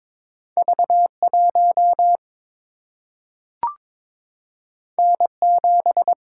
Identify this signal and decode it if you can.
Morse code, heard as a keyed tone, about 22 wpm, 700 Hz. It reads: V 1 N 7
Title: Single-Letter Single-Number 22wpm